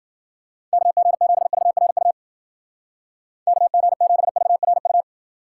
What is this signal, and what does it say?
Morse code, heard as a keyed tone, about 50 wpm, 700 Hz. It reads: KG7VRU KG7VRU
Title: Speed Racing - USA 2x3 Call Signs 40wpm